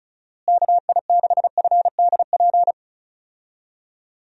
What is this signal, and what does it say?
Morse code, heard as a keyed tone, about 36 wpm, 700 Hz. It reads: KI6FDP
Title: USA Call Signs - 6 Characters - Repeated 3x - 35wpm